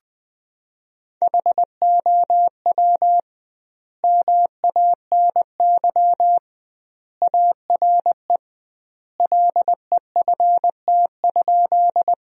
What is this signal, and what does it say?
Morse code, heard as a keyed tone, about 21 wpm, 700 Hz. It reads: HOW MANY ARE LEFT?
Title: 2x Word-Spacing - Sentences from Top 300 Words - 20wpm